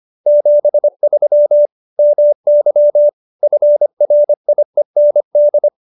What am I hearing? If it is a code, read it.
Morse code, heard as a keyed tone, about 25 wpm, 590 Hz. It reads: 73 MY FRIEND